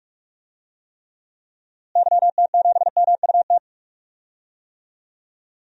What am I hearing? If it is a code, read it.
Morse code, heard as a keyed tone, about 46 wpm, 700 Hz. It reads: YT7MUT
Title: DXCC-100 Call Signs - 6 Characters - Repeated 3x - 45wpm